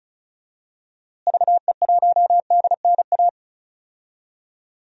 Morse code, heard as a keyed tone, about 34 wpm, 700 Hz. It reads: VE1DNA
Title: DXCC-100 Call Signs - 6 Characters - Repeated 3x - 35wpm